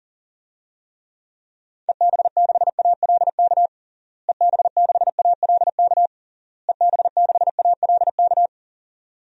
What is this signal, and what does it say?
Morse code, heard as a keyed tone, about 40 wpm, 700 Hz. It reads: EB6ALK EB6ALK EB6ALK
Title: DXCC-100 Call Signs - 6 Characters - Repeated 3x - 40wpm